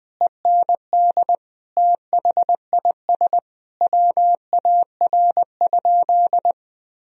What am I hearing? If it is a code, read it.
Morse code, heard as a keyed tone, about 20 wpm, 700 Hz. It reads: END THIS WAR?
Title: Sentences from Top 1000 Words - Review Part 2 - 20wpm